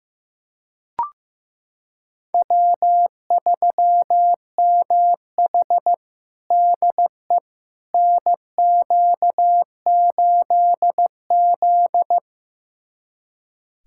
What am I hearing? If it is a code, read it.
Morse code, heard as a keyed tone, about 15 wpm, 700 Hz. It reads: W3MH DE NQ8Z
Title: USA Callsign DE Callsign - 4 Characters - 15wpm